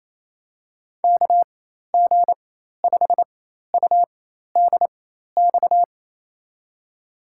Morse code, heard as a keyed tone, about 28 wpm, spaced as for 13 wpm, 700 Hz. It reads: KG5UDX
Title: USA Call Signs - 6 Characters - 4x Character Spacing - 28wpm